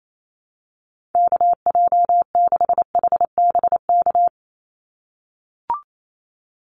Morse code, heard as a keyed tone, about 28 wpm, 700 Hz. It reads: KJ6HBK
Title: USA Call Signs 28wpm